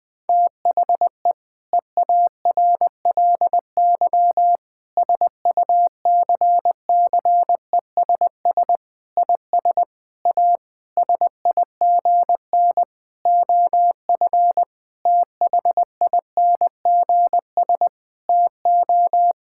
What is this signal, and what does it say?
Morse code, heard as a keyed tone, about 20 wpm, 700 Hz. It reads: THE EARLY SUCCESS IS A SIGN OF THINGS TO
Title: Sentences with 6-Word Phrases - Spoken Beginning - 20wpm